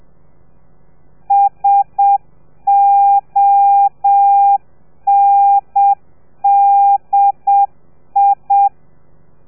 Morse code, heard as a keyed tone, about 7 wpm, 800 Hz. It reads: SONDI